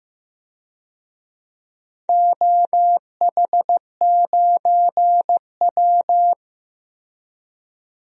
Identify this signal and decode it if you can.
Morse code, heard as a keyed tone, about 15 wpm, 700 Hz. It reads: OH9W
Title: DXCC-100 Call Signs - 4 Characters - Repeated 2x - 15wpm